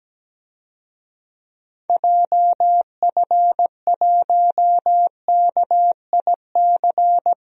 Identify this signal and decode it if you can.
Morse code, heard as a keyed tone, about 17 wpm, 700 Hz. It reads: JF1KIC
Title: DXCC-100 Call Signs - 6 Characters - Repeated 3x - 17wpm